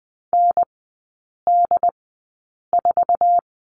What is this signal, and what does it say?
Morse code, heard as a keyed tone, about 20 wpm, spaced as for 8 wpm, 690 Hz. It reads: ND4
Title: 159 Lesson 26e - Letter g - Three Characters 20wpm